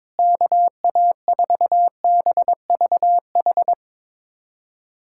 Morse code, heard as a keyed tone, about 21 wpm, 700 Hz. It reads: KA4BVH
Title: USA Callsign DE Callsign - 6 Characters - 22wpm